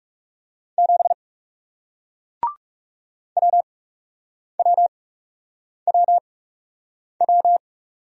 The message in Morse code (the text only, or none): 7 WWWW